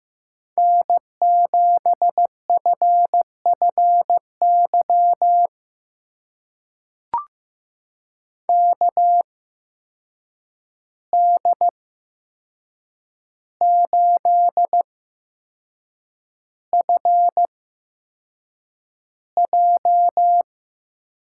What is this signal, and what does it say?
Morse code, heard as a keyed tone, about 15 wpm, spaced as for 6 wpm, 700 Hz. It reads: N7FFY KD8FJ
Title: USA Call Signs - 8x Character Spacing - 15wpm